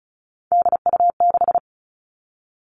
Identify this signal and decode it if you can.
Morse code, heard as a keyed tone, about 34 wpm, 700 Hz. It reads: DU6
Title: All Cut Numbers - 3-Digit 35wpm